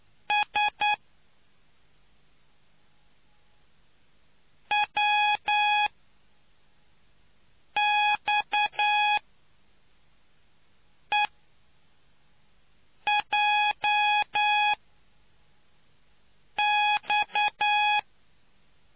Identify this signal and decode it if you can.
Morse code, heard as a keyed tone, about 9 wpm, 840 Hz. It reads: S W X E J X